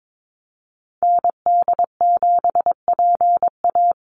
Morse code, heard as a keyed tone, about 22 wpm, 700 Hz. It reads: ND7PA
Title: NA SOTA Activator Callsigns 22wpm